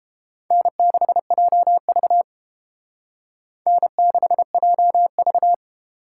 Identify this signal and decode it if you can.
Morse code, heard as a keyed tone, about 29 wpm, 700 Hz. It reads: N6JV N6JV